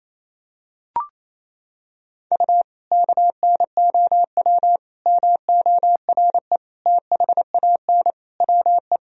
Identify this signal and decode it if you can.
Morse code, heard as a keyed tone, about 28 wpm, 700 Hz. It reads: U KNOW MORE THAN WE
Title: Sentences from Top 100 Words and Abbreviations 28wpm